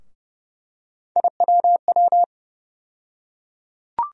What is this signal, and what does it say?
Morse code, heard as a keyed tone, about 30 wpm, 700 Hz. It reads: IWW
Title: USA Call Sign Postfixes - 3 Letters 30wpm